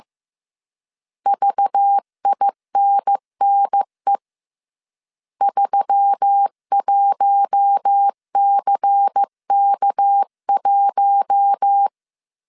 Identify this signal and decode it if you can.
Morse code, heard as a keyed tone, about 15 wpm, 800 Hz. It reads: VINNE 31CK1
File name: CW / DET / 1 6 1500C S C 14wpm 2951